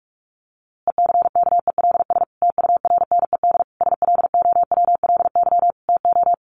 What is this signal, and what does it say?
Morse code, heard as a keyed tone, about 44 wpm, 700 Hz. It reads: ECKELS TURNED SLOWLY TO
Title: A Sound Of Thunder 45wpm